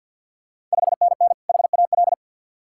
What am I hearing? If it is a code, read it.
Morse code, heard as a keyed tone, about 53 wpm, 700 Hz. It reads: 5NN HAL
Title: Ontario QSO Party Exchange 50wpm